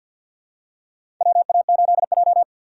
Morse code, heard as a keyed tone, about 53 wpm, 700 Hz. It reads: WA8J
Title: USA Call Signs - 3x Character Spacing - 50wpm